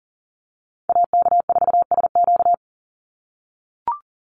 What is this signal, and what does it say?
Morse code, heard as a keyed tone, about 40 wpm, 700 Hz. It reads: AK4SQ
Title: Call Signs - 40wpm